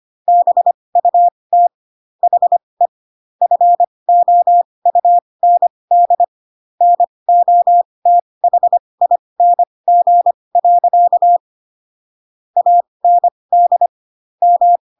Morse code, heard as a keyed tone, about 25 wpm, spaced as for 20 wpm, 700 Hz. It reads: BUT HE FOUND NOTHING. AND M